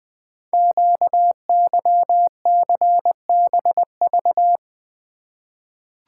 Morse code, heard as a keyed tone, about 20 wpm, 700 Hz. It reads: QYCBV
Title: Code Groups - 5 Letters - 20wpm